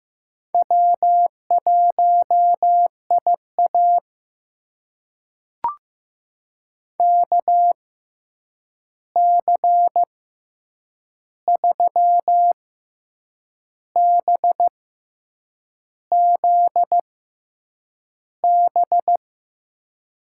Morse code, heard as a keyed tone, about 16 wpm, spaced as for 7 wpm, 700 Hz. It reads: W1IA KC3BZB
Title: USA Call Signs - 6x Character Spacing - 15wpm